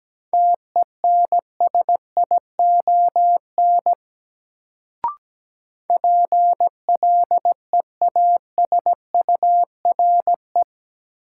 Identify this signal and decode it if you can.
Morse code, heard as a keyed tone, about 17 wpm, 700 Hz. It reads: TENSION PLEASURE
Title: Sentences from Top 1600 Words - New Word Drill - 17wpm